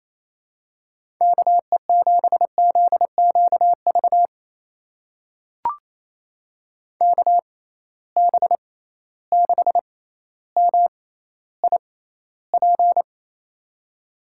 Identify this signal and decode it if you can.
Morse code, heard as a keyed tone, about 27 wpm, spaced as for 14 wpm, 700 Hz. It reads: KE7ZQV KB6MIP
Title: USA Call Signs - 6 Characters - 6x Character Spacing - 28wpm